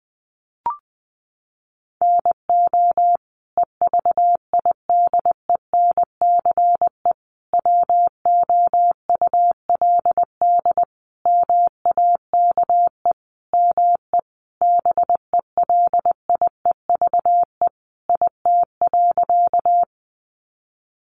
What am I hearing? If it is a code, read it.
Morse code, heard as a keyed tone, about 20 wpm, 700 Hz. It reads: NO EVIDENCE WOULD MAKE ME BELIEVE IT.
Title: Sentences from Top 1300 Words 20wpm